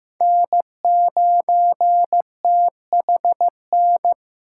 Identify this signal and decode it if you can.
Morse code, heard as a keyed tone, about 18 wpm, 700 Hz. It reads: N9THN
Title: Speed Racing - USA 1x3 Call Signs 15wpm